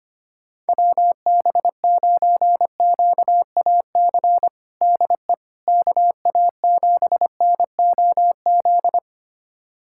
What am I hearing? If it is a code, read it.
Morse code, heard as a keyed tone, about 25 wpm, 700 Hz. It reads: WB9QAC DE KA7NOZ
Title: USA Callsign DE Callsign - 6 Characters - 25wpm